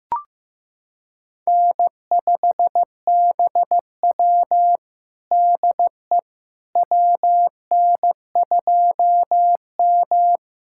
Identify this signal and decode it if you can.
Morse code, heard as a keyed tone, about 15 wpm, 700 Hz. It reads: N5BW DE WN2M
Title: USA Callsign DE Callsign - 4 Characters - 15wpm